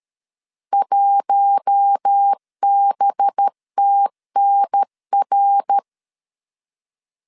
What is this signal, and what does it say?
Morse code, heard as a keyed tone, about 13 wpm, 800 Hz. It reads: 1BTNR
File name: CW / DET / 1 9 1500C S B 12wpm 3819